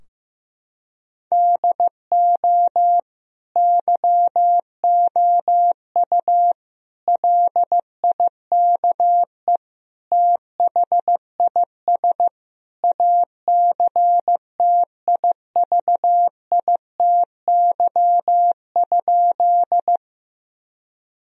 Morse code, heard as a keyed tone, about 15 wpm, 700 Hz. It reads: DO YOU LIKE THIS ACTIVITY?
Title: Sentences from Top 1000 Words - Review Part 4 - 15wpm